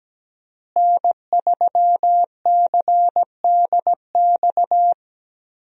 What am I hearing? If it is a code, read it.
Morse code, heard as a keyed tone, about 17 wpm, 700 Hz. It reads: N3CDX
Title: USA Call Signs - 5 Characters - Repeated 2x - 17wpm